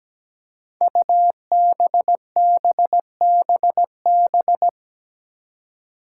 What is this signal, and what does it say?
Morse code, heard as a keyed tone, about 17 wpm, 700 Hz. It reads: UBBBB
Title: Five Challenging Letter-Number Combinations 17wpm